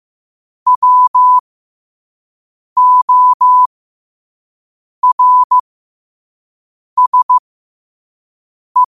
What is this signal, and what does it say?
Morse code, heard as a keyed tone, about 15 wpm, spaced as for 5 wpm, 1000 Hz. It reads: WORSE
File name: qotdfiles / 5wpm